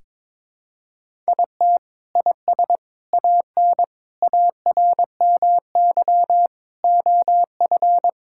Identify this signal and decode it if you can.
Morse code, heard as a keyed tone, about 22 wpm, 700 Hz. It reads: IT IS AN ARMY OF
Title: Sentences from Top 1000 Words - Review Part 5 - 22wpm